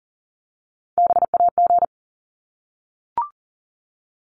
Morse code, heard as a keyed tone, about 40 wpm, 700 Hz. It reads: BAG
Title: Top 1501-1750 Words 40wpm